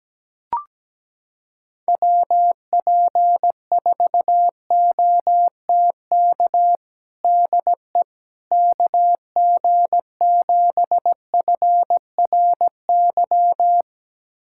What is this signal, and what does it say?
Morse code, heard as a keyed tone, about 17 wpm, 700 Hz. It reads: WP4OTK DE KG7FRY
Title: USA Callsign DE Callsign - 6 Characters - 17wpm